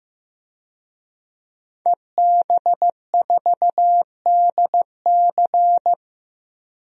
Morse code, heard as a keyed tone, about 15 wpm, 700 Hz. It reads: EB4DC
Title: DXCC-100 Call Signs - 5 Characters - 15wpm